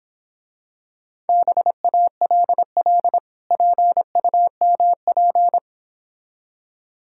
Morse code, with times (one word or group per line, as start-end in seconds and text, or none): 1.3-3.2 BALL
3.5-5.6 PUMP